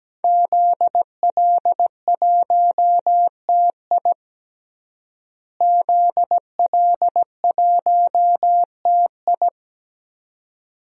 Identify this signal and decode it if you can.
Morse code, heard as a keyed tone, about 17 wpm, 700 Hz. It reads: ZL1TI ZL1TI
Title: DXCC-100 Call Signs - 5 Characters - Repeated 2x - 17wpm